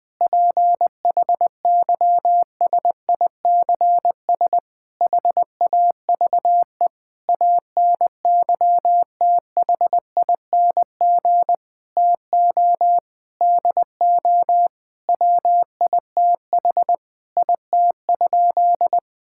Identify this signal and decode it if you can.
Morse code, heard as a keyed tone, about 20 wpm, 700 Hz. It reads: PHYSICS HAVE ANYTHING TO DO WITH IT?
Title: Sentences from Top 1000 Words - Review Part 5 - 20wpm